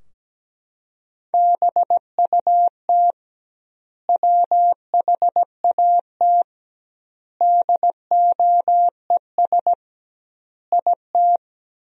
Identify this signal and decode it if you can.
Morse code, heard as a keyed tone, about 17 wpm, 700 Hz. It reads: BUT WHAT DOES IT
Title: 2x Word-Spacing - Sentences from Top 200 Words - 17wpm